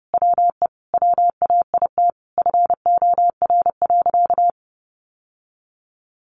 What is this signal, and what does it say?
Morse code, heard as a keyed tone, about 30 wpm, 700 Hz. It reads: WE WAIT FOR.
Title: Sentences from Top 800 Words 30wpm